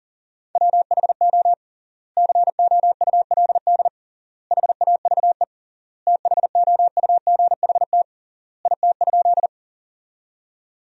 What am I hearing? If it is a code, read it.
Morse code, heard as a keyed tone, about 40 wpm, 700 Hz. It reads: WHO COULD HAVE THOUGHT IT?
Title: 3x Word-Spacing - Sentences from Top 300 Words - 40wpm